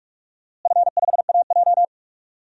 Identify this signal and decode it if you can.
Morse code, heard as a keyed tone, about 42 wpm, 700 Hz. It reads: F5AJ
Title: DXCC-100 Call Signs - 4 Characters - Repeated 3x - 45wpm